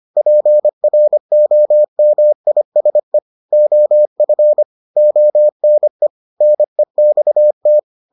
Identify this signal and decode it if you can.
Morse code, heard as a keyed tone, about 25 wpm, 600 Hz. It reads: PROMISE OF ONE NEXT